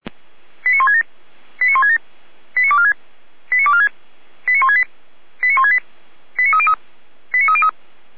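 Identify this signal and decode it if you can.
Five-tone selective call (ZVEI), heard as quick runs of five tones, tones of about 70 ms. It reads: 89167 89167 89246 89246 89168 89168 89303 89303
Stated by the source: Five-tone ZVEI selective call received on an SDR, channel 455, archived by Feuerwehr Orlamünde.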